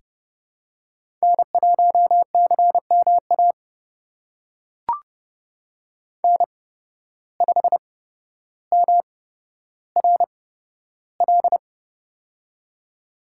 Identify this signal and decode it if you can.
Morse code, heard as a keyed tone, about 30 wpm, spaced as for 13 wpm, 700 Hz. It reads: N1CMA N5MRL
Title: USA Call Signs - 5 Characters - 8x Character Spacing - 30wpm